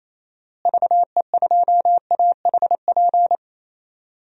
Morse code, heard as a keyed tone, about 28 wpm, 700 Hz. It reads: VE2AHP
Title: DXCC-100 Call Signs - 6 Characters - Repeated 2x - 28wpm